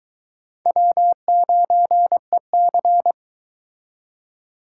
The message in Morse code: W9EC

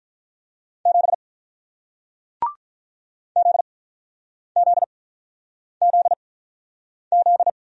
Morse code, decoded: 7 Z Z Z Z

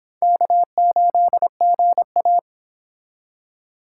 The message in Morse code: K8GA

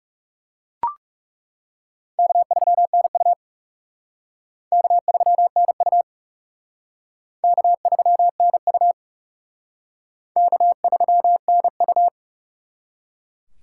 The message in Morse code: K3NU K3NU K3NU K3NU

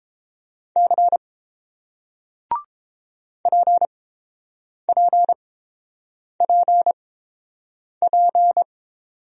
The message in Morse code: C PPPP

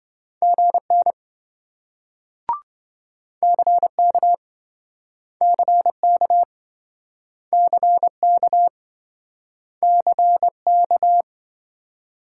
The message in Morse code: GN CK CK CK CK

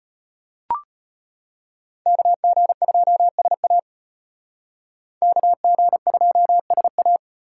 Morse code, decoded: KG2SA KG2SA

